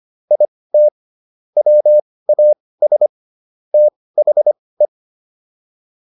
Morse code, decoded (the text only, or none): IT WAS THE